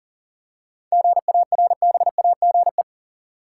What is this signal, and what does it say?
Morse code, heard as a keyed tone, about 39 wpm, 700 Hz. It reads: GARBAGE
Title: Seven Letter Words - 40wpm